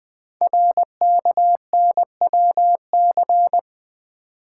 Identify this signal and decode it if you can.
Morse code, heard as a keyed tone, about 20 wpm, 700 Hz. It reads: RKNWC
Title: Code Groups - 5 Letters - 20wpm